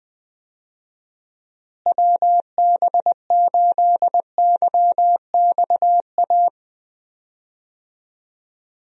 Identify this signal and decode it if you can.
Morse code, heard as a keyed tone, about 21 wpm, 700 Hz. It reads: WB8YXA
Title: USA Call Signs - 6 Characters - Repeated 2x - 20wpm